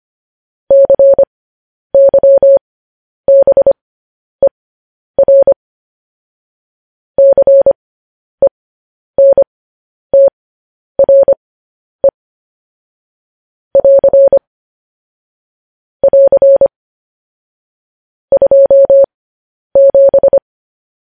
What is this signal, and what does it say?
Morse code, heard as a keyed tone, about 25 wpm, spaced as for 10 wpm, 550 Hz. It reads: CYBER CENTRE <AR> <AR> 27
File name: podcast / News headlines-1x-25x10